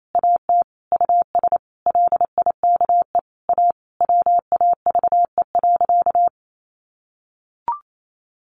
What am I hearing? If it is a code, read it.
Morse code, heard as a keyed tone, about 28 wpm, 700 Hz. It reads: AT US LIKE A WAVE.
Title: Sentences from Top 1400 Words 28wpm